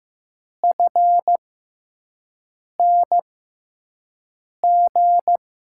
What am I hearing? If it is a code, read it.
Morse code, heard as a keyed tone, about 15 wpm, spaced as for 5 wpm, 700 Hz. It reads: FNG
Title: USA Call Signs - 6 Characters - 6x Character Spacing - 15wpm